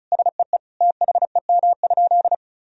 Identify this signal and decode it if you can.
Morse code, heard as a keyed tone, about 36 wpm, 700 Hz. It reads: SEE THEM?
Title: Sentences from Top 1000 Words - Review Part 1 - 35wpm